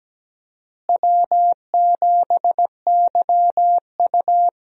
Morse code, heard as a keyed tone, about 17 wpm, 700 Hz. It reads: W7YU
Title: USA Call Signs - 4 Characters - Repeated 3x - 17wpm